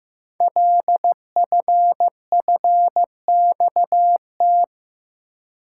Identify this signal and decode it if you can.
Morse code, heard as a keyed tone, about 15 wpm, 700 Hz. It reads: LFFXT
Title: Code Groups - 5 Letters - 15wpm